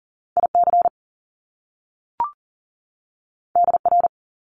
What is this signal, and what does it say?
Morse code, heard as a keyed tone, about 41 wpm, 700 Hz. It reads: IC DR
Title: Top Two Letter Combinations in Words 40wpm